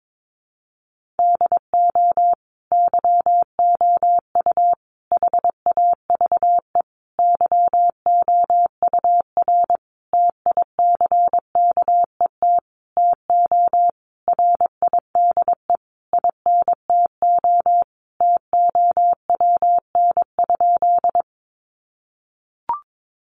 Morse code, heard as a keyed tone, about 22 wpm, 700 Hz. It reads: DO YOU HAVE YOUR TICKET TO RIDE INTO TOWN?
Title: Sentences from Top 1400 Words 22wpm